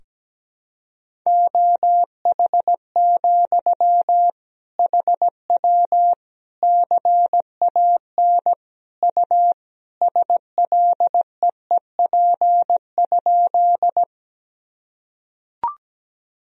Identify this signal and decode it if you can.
Morse code, heard as a keyed tone, about 17 wpm, 700 Hz. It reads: OH, HW CAN U SLEEP?
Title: Sentences from Top 500 Words and Abbreviations 17wpm